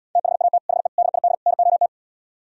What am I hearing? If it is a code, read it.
Morse code, heard as a keyed tone, about 77 wpm, 700 Hz. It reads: THAT HE DID NOT